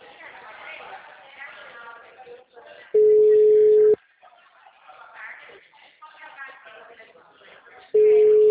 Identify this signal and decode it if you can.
Ringback tone (Europe, 425 Hz), with speech.